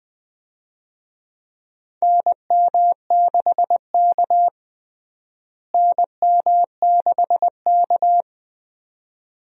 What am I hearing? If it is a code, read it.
Morse code, heard as a keyed tone, about 20 wpm, 700 Hz. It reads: NM6K NM6K
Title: USA Call Signs - 4 Characters - Repeated 2x - 20wpm